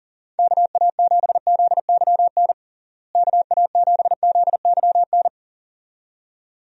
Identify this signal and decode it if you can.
Morse code, heard as a keyed tone, about 40 wpm, 700 Hz. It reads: KA7ZYN KA7ZYN
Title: USA Call Signs - 6 Characters - Repeated 2x - 40wpm